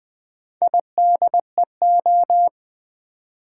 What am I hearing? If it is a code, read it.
Morse code, heard as a keyed tone, about 20 wpm, 700 Hz. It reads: IDEO
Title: Build A Word - 2 Letters At A Time - 8-Letter Words - 20wpm